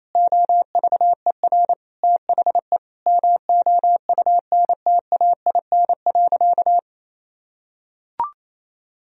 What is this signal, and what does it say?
Morse code, heard as a keyed tone, about 28 wpm, 700 Hz. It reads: OVER THE MOUNTAIN.
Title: Sentences from Top 1000 Words - Review Part 1 - 28wpm